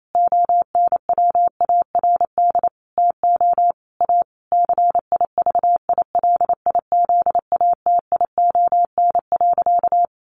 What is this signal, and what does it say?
Morse code, heard as a keyed tone, about 28 wpm, 700 Hz. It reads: ONWARD TO A CIVILIZATION.
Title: A Sound Of Thunder 28wpm